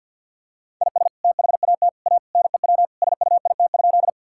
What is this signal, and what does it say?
Morse code, heard as a keyed tone, about 50 wpm, 700 Hz. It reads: IS THAT A NEW SUIT?
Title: Sentences from Top 1600 Words 50wpm